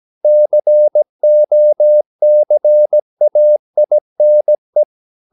Morse code, heard as a keyed tone, about 17 wpm, 600 Hz. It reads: COCAINE
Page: Romanen Never af Ken Follet på engelsk.